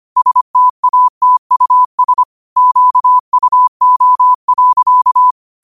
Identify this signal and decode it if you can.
Morse code, heard as a keyed tone, about 25 wpm, 1000 Hz. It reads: STATUS QUO.